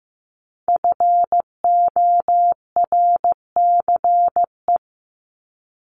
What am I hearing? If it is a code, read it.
Morse code, heard as a keyed tone, about 15 wpm, 700 Hz. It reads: FORCE
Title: ED Suffix - Top 100 Words with Base Words 15wpm